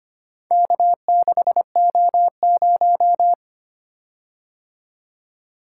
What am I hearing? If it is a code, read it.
Morse code, heard as a keyed tone, about 25 wpm, 700 Hz. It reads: K6O0
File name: practice / 25wpm / Four Letter-Number Combinations - 25wpm